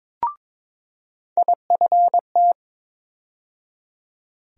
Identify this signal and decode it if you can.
Morse code, heard as a keyed tone, about 22 wpm, 700 Hz. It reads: IFT
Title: USA Call Sign Postfixes - 3 Letters 22wpm